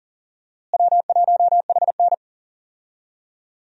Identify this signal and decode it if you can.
Morse code, heard as a keyed tone, about 40 wpm, 700 Hz. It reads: W1HN
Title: USA Call Signs - 4 Characters - Repeated 3x - 40wpm